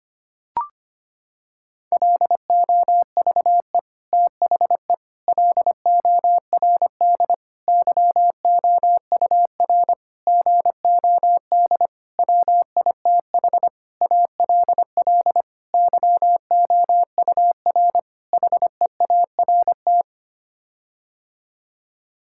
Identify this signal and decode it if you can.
Morse code, heard as a keyed tone, about 25 wpm, 700 Hz. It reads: LOVE THE LORD YOUR GOD WITH ALL YOUR HEART
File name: Sets of 9 Words - 25wpm